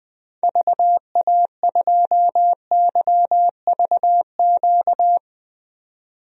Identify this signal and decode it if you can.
Morse code, heard as a keyed tone, about 20 wpm, 700 Hz. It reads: VA2YVQ
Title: DXCC-100 Call Signs - 6 Characters - 20wpm